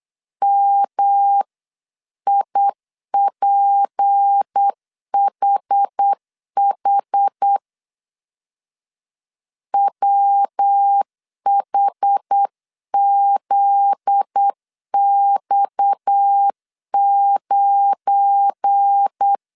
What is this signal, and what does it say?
Morse code, heard as a keyed tone, about 8 wpm, 800 Hz. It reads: MIPHH WHZX9
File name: CW / DET / 1 9 1500C S B 8wpm 5624